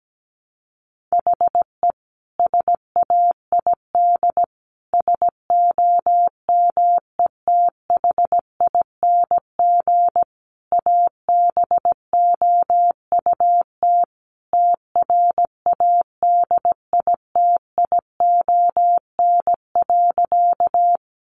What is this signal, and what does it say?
Morse code, heard as a keyed tone, about 17 wpm, 700 Hz. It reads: HE SAID SOMETHING ABOUT TRADITION.
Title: Sentences from Top 1500 Words 17wpm